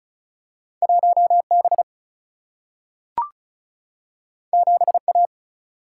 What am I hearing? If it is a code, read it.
Morse code, heard as a keyed tone, about 36 wpm, 700 Hz. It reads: 1B 7A
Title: Single-Letter Single-Number 35wpm